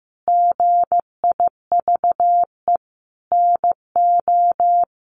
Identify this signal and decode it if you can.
Morse code, heard as a keyed tone, about 15 wpm, 700 Hz. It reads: GIVE NO
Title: Sentences from Top 1200 Words 15wpm